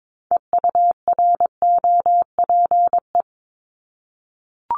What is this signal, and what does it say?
Morse code, heard as a keyed tone, about 22 wpm, 700 Hz. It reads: EUROPE
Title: Multi-letter Phonemes 22wpm